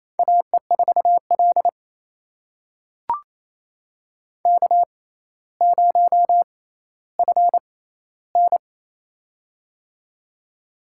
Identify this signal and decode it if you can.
Morse code, heard as a keyed tone, about 28 wpm, spaced as for 14 wpm, 700 Hz. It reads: AE4L K0FN
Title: USA Call Signs - 4 Characters - 6x Character Spacing - 28wpm